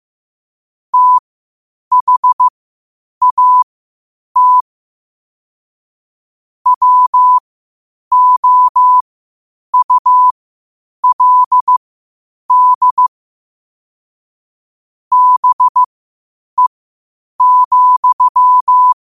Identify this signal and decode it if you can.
Morse code, heard as a keyed tone, about 15 wpm, spaced as for 8 wpm, 1000 Hz. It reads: THAT WOULD BE,
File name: qotdfiles / 8wpm